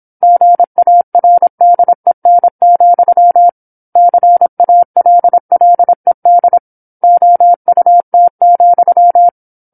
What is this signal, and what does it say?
Morse code, heard as a keyed tone, about 27 wpm, 700 Hz. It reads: GARDEN, CALLED OUT,
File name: 08-Chapter-08 26 WPM